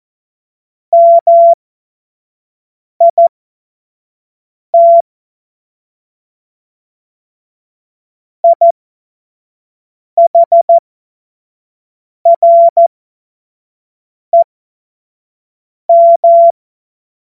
Morse code, heard as a keyed tone, about 14 wpm, spaced as for 5 wpm, 680 Hz. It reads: MIT IHREM
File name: Morsekurs / Witze / Witz 31. Mai 2022 25 F70